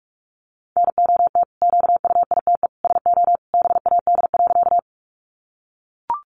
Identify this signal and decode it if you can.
Morse code, heard as a keyed tone, about 44 wpm, 700 Hz. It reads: NOT QUITE SO BAD.
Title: Sentences from Top 700 Words 45wpm